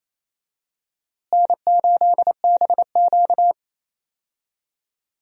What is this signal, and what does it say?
Morse code, heard as a keyed tone, about 28 wpm, 700 Hz. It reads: N8BQ
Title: USA Call Signs - 4 Characters - Repeated 3x - 28wpm